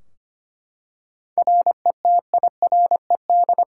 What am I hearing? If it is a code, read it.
Morse code, heard as a keyed tone, about 25 wpm, 700 Hz. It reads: RETIRED
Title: K1USN SST QSO Practice 25wpm